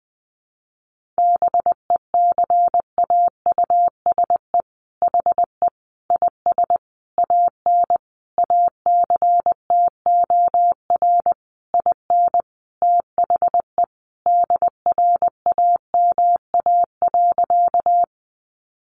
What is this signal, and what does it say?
Morse code, heard as a keyed tone, about 20 wpm, 700 Hz. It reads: BECAUSE HE IS AN ACTOR IN THE DRAMA.
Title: Sentences from Top 1500 Words 20wpm